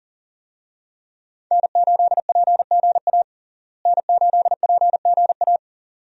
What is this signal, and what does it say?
Morse code, heard as a keyed tone, about 37 wpm, 700 Hz. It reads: N8PGA N8PGA
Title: USA Call Signs - 5 Characters - Repeated 2x - 40wpm